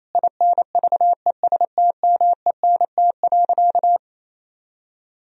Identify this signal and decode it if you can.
Morse code, heard as a keyed tone, about 28 wpm, 700 Hz. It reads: INVESTMENT.
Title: Sentences from Top 1000 Words - Review Part 4 - 28wpm